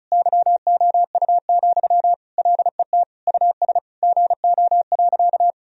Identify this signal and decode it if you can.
Morse code, heard as a keyed tone, about 36 wpm, 700 Hz. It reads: YOU, LET US GO.
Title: Sentences from Top 1000 Words - Review Part 2 - 35wpm